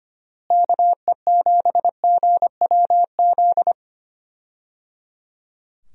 Morse code, heard as a keyed tone, about 25 wpm, 700 Hz. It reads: KE7GWZ
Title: USA Call Signs - 6 Characters - Repeated 3x - 25wpm